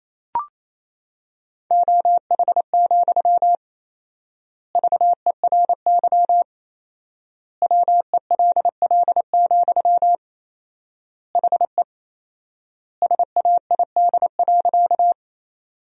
Morse code, heard as a keyed tone, about 27 wpm, 700 Hz. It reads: OH, VERY WELL, HE SAID.
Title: 4x Word-Spacing - Sentences from Top 500 Words - 28wpm